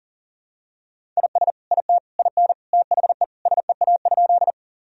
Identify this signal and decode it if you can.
Morse code, heard as a keyed tone, about 40 wpm, 700 Hz. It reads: IS IT IN THE SEA?
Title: Sentences from Top 1000 Words - Review Part 1 - 40wpm